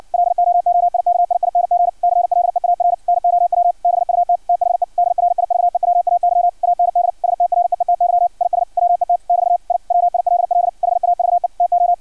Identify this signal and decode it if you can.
Morse code, heard as a keyed tone, about 56 wpm, 700 Hz. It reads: LOOIKIING CLEAR NOW BUT THE DRIVEWAY AND STREETY IS WET <BT> I WILL HAVE TO
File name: archived / CRSnet